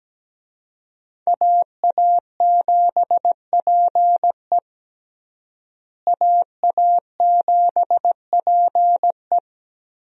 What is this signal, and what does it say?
Morse code, heard as a keyed tone, about 17 wpm, 700 Hz. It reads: AA7PE AA7PE
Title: USA Call Signs - 5 Characters - Repeated 3x - 17wpm